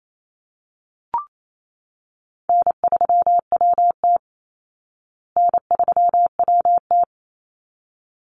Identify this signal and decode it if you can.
Morse code, heard as a keyed tone, about 28 wpm, 700 Hz. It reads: N3WT N3WT
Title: Call Signs Repeated 3x - 28wpm